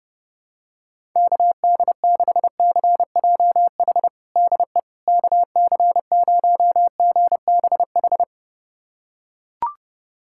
Morse code, heard as a keyed tone, about 30 wpm, 700 Hz. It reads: KD6CJH DE KC0GBH
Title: USA Callsign DE Callsign 30wpm